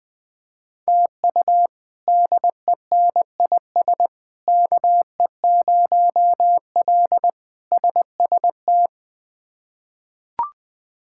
Morse code, heard as a keyed tone, about 20 wpm, 700 Hz. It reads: TU DENIS KE0L SST
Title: K1USN SST QSO Practice - 20wpm